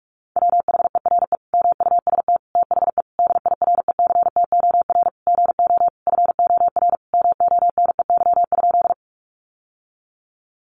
Morse code, heard as a keyed tone, about 44 wpm, 700 Hz. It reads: WHERE MUST THE DIRECTOR GO FOR MONEY?
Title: Sentences from Top 1300 Words 45wpm